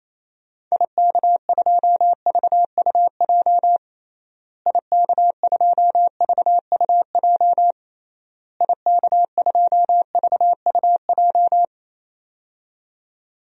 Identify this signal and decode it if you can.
Morse code, heard as a keyed tone, about 28 wpm, 700 Hz. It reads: IK2VUJ IK2VUJ IK2VUJ
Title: DXCC-100 Call Signs - 6 Characters - Repeated 3x - 28wpm